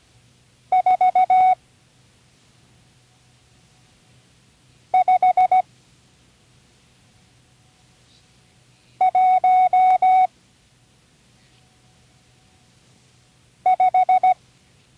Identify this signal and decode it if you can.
Morse code, heard as a keyed tone, about 17 wpm, 740 Hz. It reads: 4 5 1 5
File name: CW Learning Programs / Learning Morse Code (from Tune in the World) / 34 random numbers 1-5